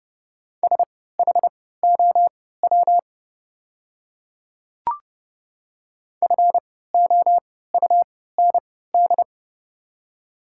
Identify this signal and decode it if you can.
Morse code, heard as a keyed tone, about 30 wpm, spaced as for 17 wpm, 700 Hz. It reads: SHOW FOUND
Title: 3x Character Spacing - Top 300 Words - 30wpm